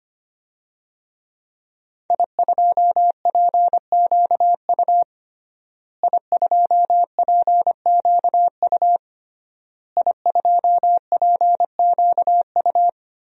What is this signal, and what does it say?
Morse code, heard as a keyed tone, about 25 wpm, 700 Hz. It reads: I2PQU I2PQU I2PQU
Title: DXCC-100 Call Signs - 5 Characters - Repeated 3x - 25wpm